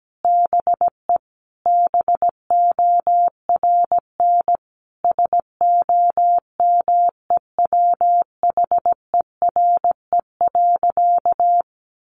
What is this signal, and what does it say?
Morse code, heard as a keyed tone, about 17 wpm, 700 Hz. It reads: BE BORN SOMEWHERE.
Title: Sentences from Top 1400 Words 17wpm